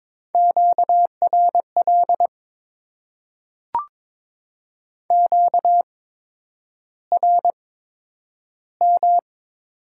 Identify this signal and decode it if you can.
Morse code, heard as a keyed tone, about 22 wpm, spaced as for 9 wpm, 700 Hz. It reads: QRL QRM
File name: Top 90 QSO Elements - 8x Character Spacing - 22wpm